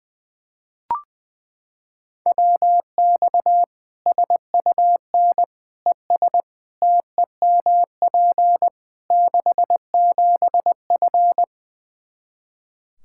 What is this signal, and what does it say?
Morse code, heard as a keyed tone, about 21 wpm, 700 Hz. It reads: WX SUN ES TEMP 67F
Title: Practice QSO - Exchange 2 - Info - 20wpm